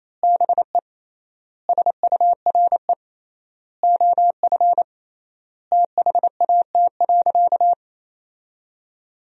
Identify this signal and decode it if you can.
Morse code, heard as a keyed tone, about 28 wpm, 700 Hz. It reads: BE SURE OF THAT.
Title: 3x Word-Spacing - Sentences from Top 300 Words - 28wpm